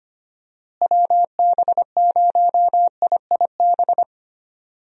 Morse code, heard as a keyed tone, about 25 wpm, 700 Hz. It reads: WB0IIB
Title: USA Call Signs - 6 Characters - Repeated 3x - 25wpm